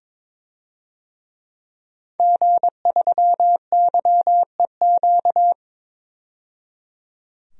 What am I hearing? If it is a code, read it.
Morse code, heard as a keyed tone, about 21 wpm, 700 Hz. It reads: G3YEQ
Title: DXCC-100 Call Signs - 5 Characters - 22wpm